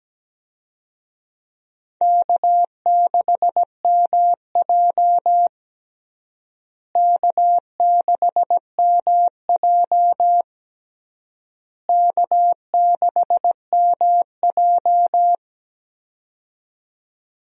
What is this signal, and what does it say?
Morse code, heard as a keyed tone, about 17 wpm, 700 Hz. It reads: K6MJ K6MJ K6MJ
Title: USA Call Signs - 4 Characters - Repeated 3x - 17wpm